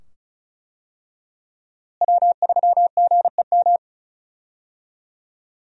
Morse code, heard as a keyed tone, about 34 wpm, 700 Hz. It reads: W3GEM
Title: USA Call Signs - 5 Characters - Repeated 2x - 35wpm